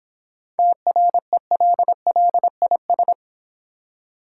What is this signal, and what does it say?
Morse code, heard as a keyed tone, about 24 wpm, 700 Hz. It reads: TRELLIS
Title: Easy Words - Speed-Racing with Hints - 20wpm